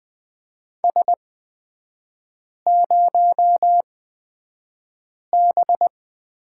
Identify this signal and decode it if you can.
Morse code, heard as a keyed tone, about 20 wpm, spaced as for 5 wpm, 700 Hz. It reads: S0B
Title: 229 Lesson 37e - Letter x - Three Characters 20wpm